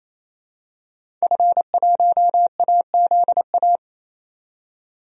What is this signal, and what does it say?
Morse code, heard as a keyed tone, about 28 wpm, 700 Hz. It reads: F1AZA
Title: DXCC-100 Call Signs - 5 Characters - 28wpm